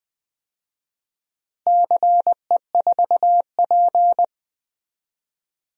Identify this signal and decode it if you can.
Morse code, heard as a keyed tone, about 20 wpm, 700 Hz. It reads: CE4P
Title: DXCC-100 Call Signs - 4 Characters - Repeated 2x - 20wpm